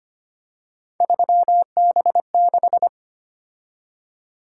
Morse code, heard as a keyed tone, about 29 wpm, 700 Hz. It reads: 3B6
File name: Speed-Racing - Three Letter-Number Combinations - 25wpm